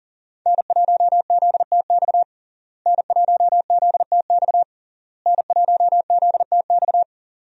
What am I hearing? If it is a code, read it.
Morse code, heard as a keyed tone, about 40 wpm, 700 Hz. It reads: N1ZTX N1ZTX N1ZTX
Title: USA Call Signs - 5 Characters - Repeated 3x - 40wpm